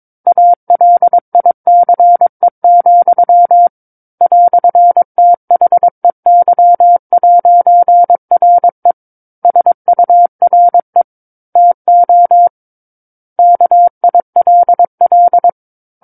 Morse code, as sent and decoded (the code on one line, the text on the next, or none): .- .-.. .. -.-. --..-- / .-..-. - .... -.-- .----. .-. / ... ..- .-. / - --- / -.- .. .-.. .-..
ALICE, "THEY'RE SURE TO KILL